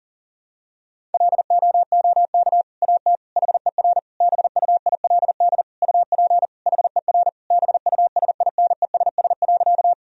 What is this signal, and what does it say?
Morse code, heard as a keyed tone, about 40 wpm, 700 Hz. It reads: LOOK AT HER BUILD UP HER BUSINESS.
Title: Sentences from Top 1000 Words - Review Part 1 - 40wpm